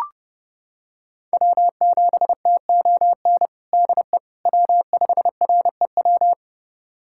Morse code, heard as a keyed tone, about 32 wpm, 700 Hz. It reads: W7TON DE W5REW
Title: USA Callsign DE Callsign - 5 Characters - 30wpm